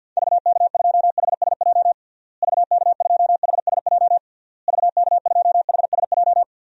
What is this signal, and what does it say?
Morse code, heard as a keyed tone, about 52 wpm, 700 Hz. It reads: VK2HSJ VK2HSJ VK2HSJ
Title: DXCC-100 Call Signs - 6 Characters - Repeated 3x - 50wpm